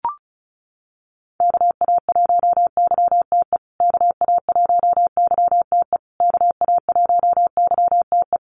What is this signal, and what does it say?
Morse code, heard as a keyed tone, about 34 wpm, 700 Hz. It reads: KA1YTE KA1YTE KA1YTE
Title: USA Call Signs Repeated 3x 35wpm